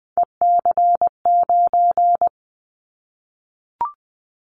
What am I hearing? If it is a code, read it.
Morse code, heard as a keyed tone, about 20 wpm, 700 Hz. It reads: EC9
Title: Callsign Prefixes for Top 100 DXCC - 20wpm